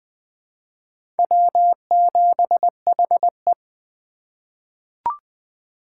Morse code, heard as a keyed tone, about 20 wpm, 700 Hz. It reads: W7HE
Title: USA Call Signs - 4 Characters - 6x Character Spacing - 20wpm